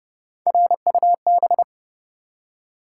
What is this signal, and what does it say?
Morse code, heard as a keyed tone, about 28 wpm, 700 Hz. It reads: RUB
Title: Three Letter Words - 30wpm